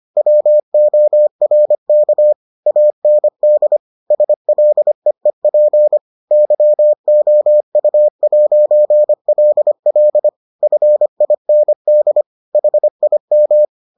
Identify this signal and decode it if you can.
Morse code, heard as a keyed tone, about 25 wpm, 600 Hz. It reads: WORK AND SLEEP YOU'LL FIND HIM